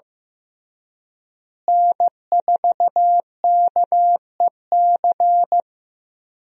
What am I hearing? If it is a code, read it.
Morse code, heard as a keyed tone, about 16 wpm, 700 Hz. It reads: N4KEC
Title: USA Call Signs - 5 Characters - Repeated 3x - 15wpm